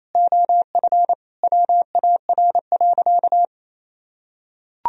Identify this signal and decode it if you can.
Morse code, heard as a keyed tone, about 28 wpm, 700 Hz. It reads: OF WAR.
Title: Sentences from Top 1000 Words - Review Part 3 - 28wpm